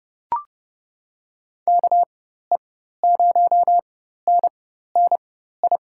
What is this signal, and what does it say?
Morse code, heard as a keyed tone, about 31 wpm, spaced as for 14 wpm, 700 Hz. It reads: KE0NNI
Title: USA Call Signs - 6 Characters - 4x Character Spacing - 30wpm